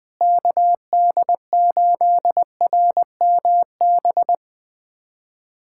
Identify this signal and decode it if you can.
Morse code, heard as a keyed tone, about 21 wpm, 700 Hz. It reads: KD8RMB
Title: USA Callsign DE Callsign 20wpm